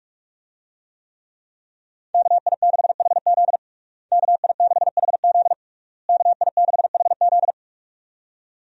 Morse code, heard as a keyed tone, about 46 wpm, 700 Hz. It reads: KI6HZ KI6HZ KI6HZ
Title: USA Call Signs - 5 Characters - Repeated 3x - 45wpm